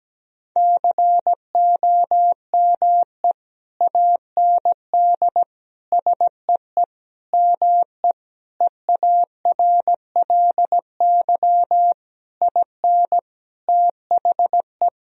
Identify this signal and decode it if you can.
Morse code, heard as a keyed tone, about 17 wpm, 700 Hz. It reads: COME AND SEE ME EARLY IN THE
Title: Sentences from Top 1000 Words - Review Part 2 - 17wpm